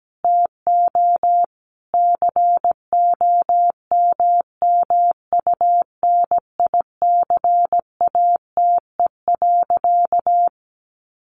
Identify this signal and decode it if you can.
Morse code, heard as a keyed tone, about 17 wpm, 700 Hz. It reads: TO COMMUNICATE.
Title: Sentences from Top 1300 Words 17wpm